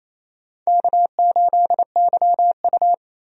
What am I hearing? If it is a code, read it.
Morse code, heard as a keyed tone, about 28 wpm, 700 Hz. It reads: K8YU
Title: USA Call Signs - 4 Characters - 28wpm